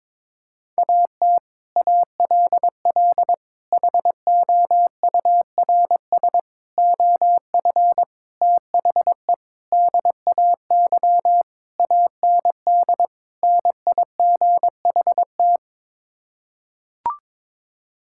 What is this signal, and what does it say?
Morse code, heard as a keyed tone, about 21 wpm, 700 Hz. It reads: AT ALL HOURS OF THE DAY AND NIGHT
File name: practice / 22wpm / Sets of 8 Words - 22wpm